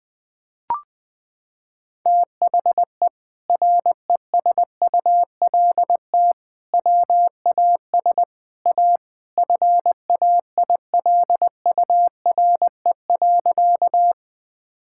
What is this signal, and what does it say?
Morse code, heard as a keyed tone, about 20 wpm, 700 Hz. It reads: THE RESULT WAS A FAILURE.
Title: Sentences from Top 1000 Words - Review Part 6 - 20wpm